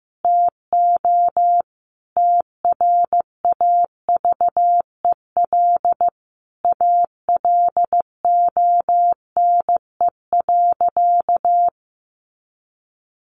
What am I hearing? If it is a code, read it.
Morse code, heard as a keyed tone, about 15 wpm, 700 Hz. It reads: TO TRAVEL ALONE.
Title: Sentences from Top 1100 Words 15wpm